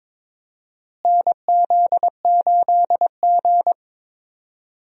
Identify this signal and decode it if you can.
Morse code, heard as a keyed tone, about 22 wpm, 700 Hz. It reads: NZ8G
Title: USA Call Signs - 4 Characters - Repeated 3x - 22wpm